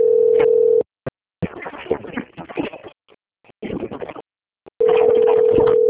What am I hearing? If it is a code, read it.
Ringback tone (North America), 440 + 480 Hz.
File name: out- - - - - .46830